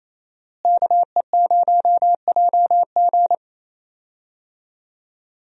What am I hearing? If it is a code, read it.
Morse code, heard as a keyed tone, about 27 wpm, 700 Hz. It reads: KE0JG
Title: USA Call Signs - 5 Characters - Repeated 3x - 28wpm